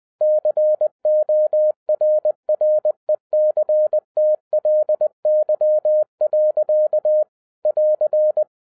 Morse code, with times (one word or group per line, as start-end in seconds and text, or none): 0.2-7.2 CORRECTLY.
7.7-8.4 <AR>